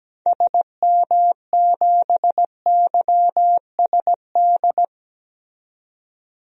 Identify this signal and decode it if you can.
Morse code, heard as a keyed tone, about 17 wpm, 700 Hz. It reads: SM7YSD
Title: DXCC-100 Call Signs - 6 Characters - 17wpm